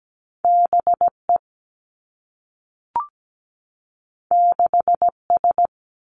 Morse code, heard as a keyed tone, about 17 wpm, 700 Hz. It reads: BE 6S
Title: Two Challenging Letter-Number Combinations 17wpm